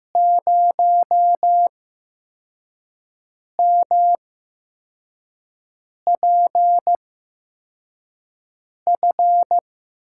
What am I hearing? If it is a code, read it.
Morse code, heard as a keyed tone, about 15 wpm, spaced as for 4 wpm, 700 Hz. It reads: 0MPF
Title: USA Call Signs - 6 Characters - 8x Character Spacing - 15wpm